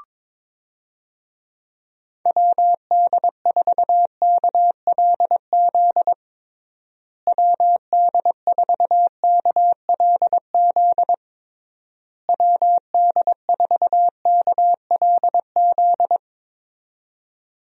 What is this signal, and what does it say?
Morse code, heard as a keyed tone, about 22 wpm, 700 Hz. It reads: WD4KLZ WD4KLZ WD4KLZ
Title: USA Call Signs - 6 Characters - Repeated 3x - 22wpm